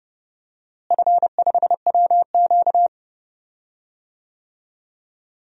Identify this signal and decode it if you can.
Morse code, heard as a keyed tone, about 30 wpm, 700 Hz. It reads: F5WQ
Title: DXCC-100 Call Signs - 4 Characters - Repeated 3x - 30wpm